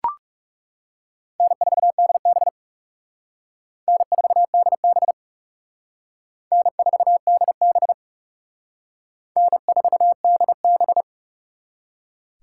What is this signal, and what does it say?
Morse code, heard as a keyed tone, about 37 wpm, 700 Hz. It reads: N4DB N4DB N4DB N4DB
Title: Speed Racing - USA 1x2 Call Signs 30wpm